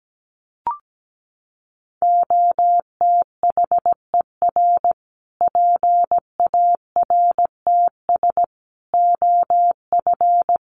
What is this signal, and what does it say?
Morse code, heard as a keyed tone, about 17 wpm, 700 Hz. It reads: OTHER PARTS OF
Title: Sets of 5 Words Encore 17wpm